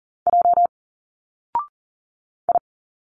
Morse code, heard as a keyed tone, about 39 wpm, 700 Hz. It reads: J I
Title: Single Letters 40wpm